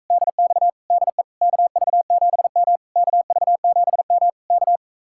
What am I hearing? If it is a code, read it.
Morse code, heard as a keyed tone, about 44 wpm, 700 Hz. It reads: DX DE KV7M KV7M K